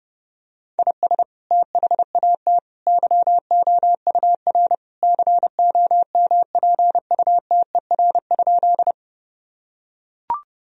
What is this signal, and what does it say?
Morse code, heard as a keyed tone, about 32 wpm, 700 Hz. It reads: IS THAT YOUR COMPUTER?
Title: Sentences from Top 1000 Words - Review Part 3 - 30wpm